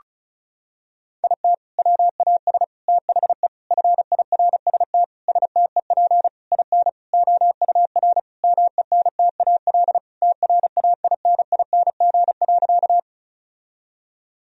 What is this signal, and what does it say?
Morse code, heard as a keyed tone, about 36 wpm, 700 Hz. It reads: IT WAS THE FIRST STEP IN OUR MENTAL TRAINING.
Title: Sentences from Top 1000 Words - Review Part 4 - 35wpm